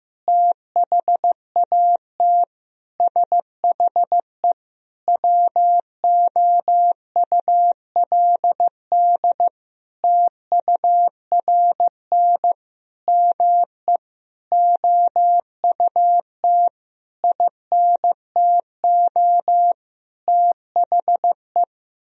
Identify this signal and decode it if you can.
Morse code, heard as a keyed tone, about 15 wpm, 700 Hz. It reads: THAT SHE WOULD TURN ME OUT INTO THE
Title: Sentences from Top 1000 Words - Review Part 4 - 15wpm